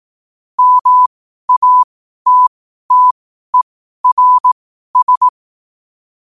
Morse code, heard as a keyed tone, about 18 wpm, spaced as for 12 wpm, 1000 Hz. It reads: MATTERS
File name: qotdfiles / 12wpm